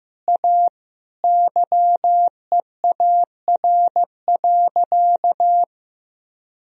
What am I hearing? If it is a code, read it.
Morse code, heard as a keyed tone, about 15 wpm, 700 Hz. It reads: A YEAR.
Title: Sentences from Top 1000 Words - Review Part 2 - 15wpm